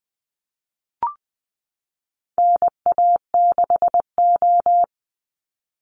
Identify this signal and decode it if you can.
Morse code, heard as a keyed tone, about 20 wpm, 700 Hz. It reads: NA6O
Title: NA SOTA Chaser Callsigns 20wpm